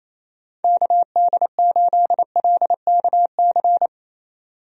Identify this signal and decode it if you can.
Morse code, heard as a keyed tone, about 27 wpm, 700 Hz. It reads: KD8LKC
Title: USA Call Signs - 6 Characters - 28wpm